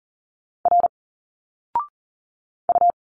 Morse code, heard as a keyed tone, about 41 wpm, 700 Hz. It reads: R U
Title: Single Letters 40wpm